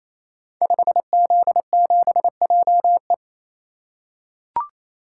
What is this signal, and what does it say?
Morse code, heard as a keyed tone, about 29 wpm, 700 Hz. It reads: 5Z7JE
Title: Code Groups - 5 Letter:Numbers - 28wpm